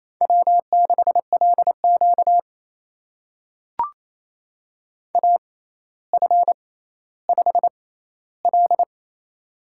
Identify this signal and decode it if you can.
Morse code, heard as a keyed tone, about 27 wpm, spaced as for 14 wpm, 700 Hz. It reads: W6LQ AF5L
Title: USA Call Signs - 4 Characters - 6x Character Spacing - 28wpm